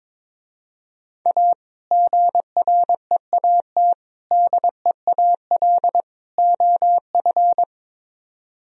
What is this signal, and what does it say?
Morse code, heard as a keyed tone, about 22 wpm, 700 Hz. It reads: A GREAT DEAL OF
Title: Speed-Racing - Sets of 4 Words 22wpm